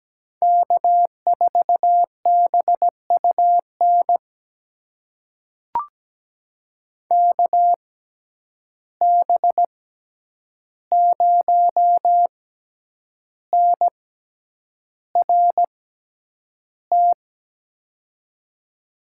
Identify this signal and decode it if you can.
Morse code, heard as a keyed tone, about 17 wpm, spaced as for 8 wpm, 700 Hz. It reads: K4BUN KB0NRT
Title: USA Call Signs - 6x Character Spacing - 17wpm